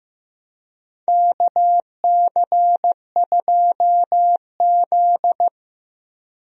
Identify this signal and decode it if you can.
Morse code, heard as a keyed tone, about 15 wpm, 700 Hz. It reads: KC2Z
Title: USA Call Signs - 4 Characters - Repeated 2x - 15wpm